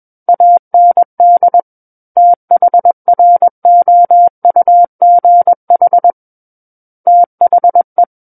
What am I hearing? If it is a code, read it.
Morse code, heard as a keyed tone, about 21 wpm, 700 Hz. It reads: AND THROUGH THE
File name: Book One / 12-Chapter 12 - 21 WPM